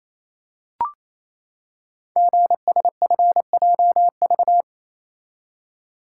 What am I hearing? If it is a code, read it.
Morse code, heard as a keyed tone, about 27 wpm, 700 Hz. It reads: GSFJV
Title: Code Groups - 5 Letters - 28wpm